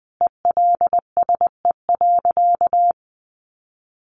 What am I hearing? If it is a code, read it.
Morse code, heard as a keyed tone, about 20 wpm, 700 Hz. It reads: ELSE.
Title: Sentences from Top 1500 Words 20wpm